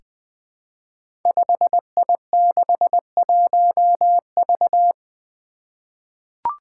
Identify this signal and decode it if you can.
Morse code, heard as a keyed tone, about 20 wpm, 700 Hz. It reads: 5I61V
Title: Five Challenging Letter-Number Combinations 20wpm